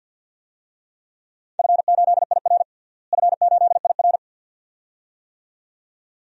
Morse code, heard as a keyed tone, about 50 wpm, 700 Hz. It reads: F8IR F8IR
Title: DXCC-100 Call Signs - 4 Characters - Repeated 2x - 50wpm